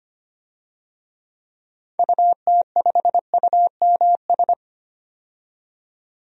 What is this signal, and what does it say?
Morse code, heard as a keyed tone, about 25 wpm, 700 Hz. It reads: UT5UMS
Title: DXCC-100 Call Signs - 6 Characters - 25wpm